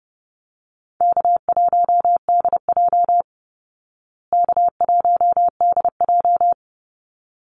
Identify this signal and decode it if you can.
Morse code, heard as a keyed tone, about 30 wpm, 700 Hz. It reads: K1DJ K1DJ
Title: Easy Call Signs Repeated 2x - 30wpm